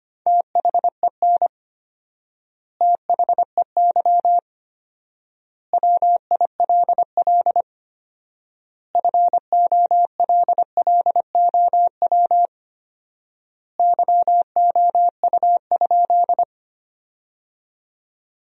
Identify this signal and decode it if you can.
Morse code, heard as a keyed tone, about 25 wpm, 700 Hz. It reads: THEN THEY WILL FOLLOW YOU?
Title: 4x Word-Spacing - Sentences from Top 200 Words - 25wpm